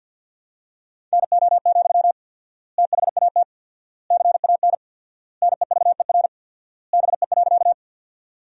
Morse code, heard as a keyed tone, about 47 wpm, 700 Hz. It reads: NO, THAT CAN NEVER BE.
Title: 4x Word-Spacing - Sentences from Top 300 Words - 50wpm